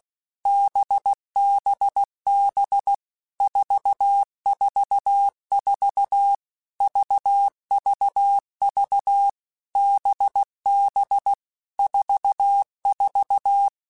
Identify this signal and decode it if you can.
Morse code, heard as a keyed tone, about 16 wpm, 800 Hz. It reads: BBB 444 VVV BB 44
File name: Basic Letter Groups By7Groups / VB46 / VB46@15